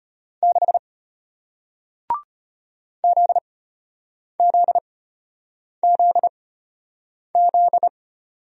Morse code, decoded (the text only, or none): 6 ZZZZ